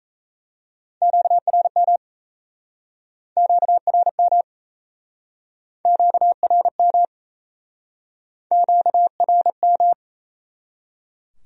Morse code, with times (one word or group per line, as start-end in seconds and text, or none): 1.0-2.0 QRM
3.4-4.4 QRM
5.9-7.0 QRM
8.5-9.9 QRM